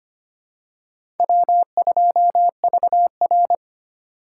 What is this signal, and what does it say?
Morse code, heard as a keyed tone, about 25 wpm, 700 Hz. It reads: W2VR
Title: USA Call Signs - 4 Characters - 4x Character Spacing - 25wpm